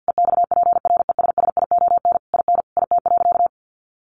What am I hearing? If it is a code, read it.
Morse code, heard as a keyed tone, about 50 wpm, 700 Hz. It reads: EXPRESSION IN IT.
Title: Sentences from Top 1200 Words 50wpm